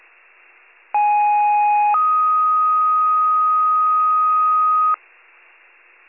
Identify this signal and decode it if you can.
Two-tone page, heard as a one-second tone, then a three-second tone, about 830 then 1250 Hz. It is heard from 0.9 to 5.0 s.